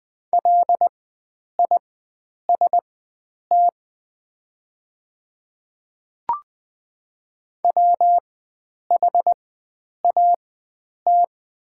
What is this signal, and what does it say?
Morse code, heard as a keyed tone, about 20 wpm, spaced as for 9 wpm, 700 Hz. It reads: LIST WHAT
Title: 4x Character Spacing - Top 500 Words - 20wpm